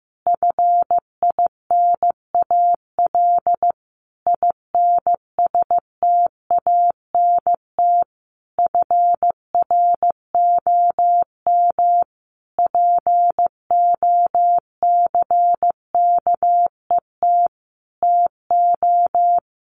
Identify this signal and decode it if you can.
Morse code, heard as a keyed tone, about 15 wpm, 700 Hz. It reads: FINAL INSTANT FROM POCKET TO
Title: A Sound Of Thunder 15wpm